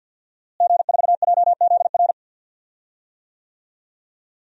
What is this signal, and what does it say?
Morse code, heard as a keyed tone, about 46 wpm, 700 Hz. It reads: G4JZR
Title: DXCC-100 Call Signs - 5 Characters - Repeated 3x - 50wpm